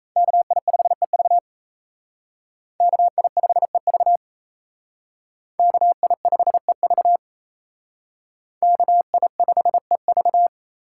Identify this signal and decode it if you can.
Morse code, heard as a keyed tone, about 35 wpm, 700 Hz. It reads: KI5EV KI5EV KI5EV KI5EV
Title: Speed Racing - USA 2x2 Call Signs 28wpm